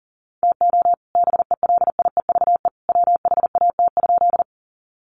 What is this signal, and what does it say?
Morse code, heard as a keyed tone, about 40 wpm, 700 Hz. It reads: TO BELIEVE WHAT?
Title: Sentences from Top 1000 Words 40wpm